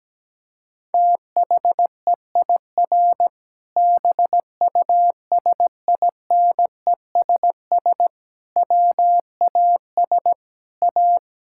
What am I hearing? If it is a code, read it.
Morse code, heard as a keyed tone, about 17 wpm, 700 Hz. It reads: THEIR BUSINESS WAS A
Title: Sentences from Top 1000 Words - Review Part 6 - 17wpm